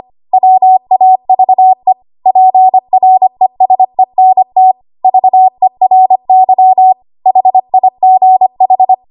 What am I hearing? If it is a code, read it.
Morse code, heard as a keyed tone, about 25 wpm, 750 Hz. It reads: WAVE PRESENT VERY HIGH
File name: Archive / 25 WPM / 25WPM